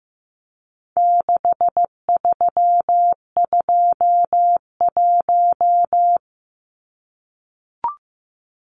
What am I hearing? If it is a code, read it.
Morse code, heard as a keyed tone, about 15 wpm, 700 Hz. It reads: 6321
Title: Numbers - 4-Digit 15wpm